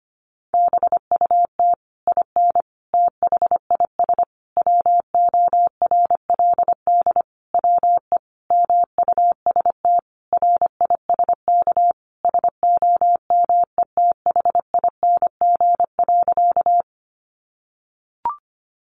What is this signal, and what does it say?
Morse code, heard as a keyed tone, about 25 wpm, 700 Hz. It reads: BUT IN THIS WORLD WE MUST RISK SOMETHING.
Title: Sentences from Top 600 Words 25wpm